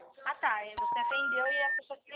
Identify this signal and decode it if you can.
Special information tone, with speech.
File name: out- -2011- - - .14679